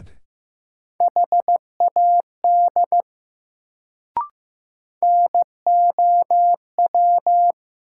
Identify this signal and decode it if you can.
Morse code, heard as a keyed tone, about 15 wpm, 700 Hz. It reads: HAD NOW
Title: Sets of 2 Words - New Word Drill - 15wpm